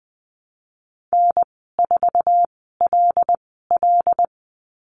Morse code, heard as a keyed tone, about 20 wpm, spaced as for 15 wpm, 700 Hz. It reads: N4LL
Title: 154 Lesson 25g - Letter p - Call Signs 20wpm